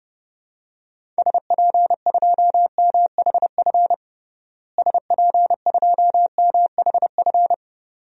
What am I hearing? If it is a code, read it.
Morse code, heard as a keyed tone, about 30 wpm, 700 Hz. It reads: SP2MHF SP2MHF
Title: DXCC-100 Call Signs - 6 Characters - Repeated 2x - 30wpm